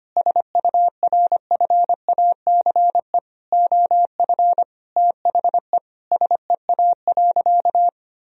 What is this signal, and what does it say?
Morse code, heard as a keyed tone, about 25 wpm, 700 Hz. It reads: SURFACE OF THE SEA.
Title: Sentences from Top 1000 Words - Review Part 3 - 25wpm